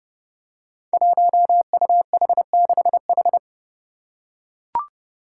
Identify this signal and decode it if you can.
Morse code, heard as a keyed tone, about 32 wpm, 700 Hz. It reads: 1UH6H
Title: Five Challenging Letter-Number Combinations 30wpm